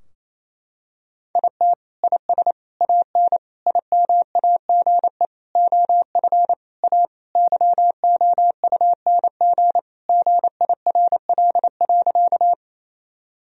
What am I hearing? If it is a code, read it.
Morse code, heard as a keyed tone, about 28 wpm, 700 Hz. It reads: IT IS AN IMAGE OF A YOUNG GIRL.
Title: Sentences from Top 1000 Words - Review Part 4 - 28wpm